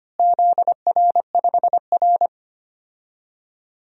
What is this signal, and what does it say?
Morse code, heard as a keyed tone, about 25 wpm, 700 Hz. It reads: ZR5R
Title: DXCC-100 Call Signs - 4 Characters - Repeated 3x - 25wpm